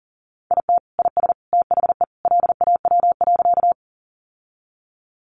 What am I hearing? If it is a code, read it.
Morse code, heard as a keyed tone, about 38 wpm, 700 Hz. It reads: IT IS THE LAW.
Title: Sentences from Top 600 Words 40wpm